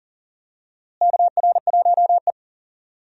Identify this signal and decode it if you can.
Morse code, heard as a keyed tone, about 37 wpm, 700 Hz. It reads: KR1E
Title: USA Call Signs - 4 Characters - Repeated 2x - 40wpm